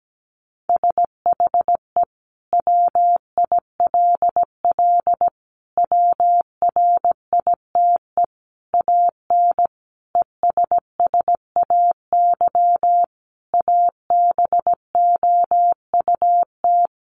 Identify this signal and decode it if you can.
Morse code, heard as a keyed tone, about 17 wpm, 700 Hz. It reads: SHE WILL WRITE AN ESSAY ABOUT
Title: Sentences from Top 1500 Words 17wpm